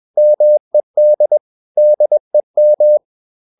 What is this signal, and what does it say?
Morse code, heard as a keyed tone, about 21 wpm, 600 Hz. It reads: MED DEM
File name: Ferie QRS 21wpm